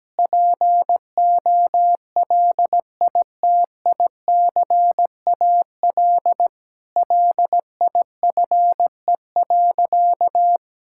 Morse code, heard as a keyed tone, about 17 wpm, 700 Hz. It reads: POLITICAL LIFE.
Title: Sentences from Top 1000 Words - Review Part 5 - 17wpm